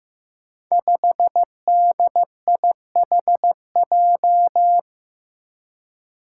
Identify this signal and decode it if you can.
Morse code, heard as a keyed tone, about 15 wpm, 700 Hz. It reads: 5DIHJ
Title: Five Challenging Letter-Number Combinations 15wpm